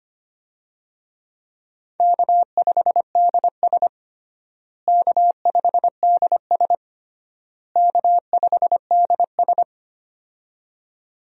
Morse code, heard as a keyed tone, about 25 wpm, 700 Hz. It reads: K5DS K5DS K5DS
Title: USA Call Signs - 4 Characters - Repeated 3x - 25wpm